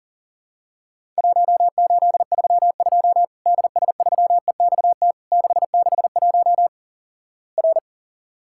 Morse code, heard as a keyed tone, about 43 wpm, 700 Hz. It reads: 1832 DS3EXT 661 R
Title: Worked All Europe DX Contest - QTC Exchange - 40wpm